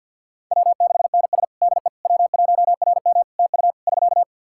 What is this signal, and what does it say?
Morse code, heard as a keyed tone, about 47 wpm, 700 Hz. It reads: W6NS DE W1RM TU <SK>
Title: Practice QSO - Exchange 4 - Ending - 50wpm